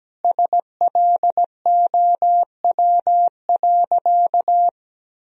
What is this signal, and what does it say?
Morse code, heard as a keyed tone, about 17 wpm, 700 Hz. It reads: SLOW.
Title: Sentences from Top 1000 Words - Review Part 2 - 17wpm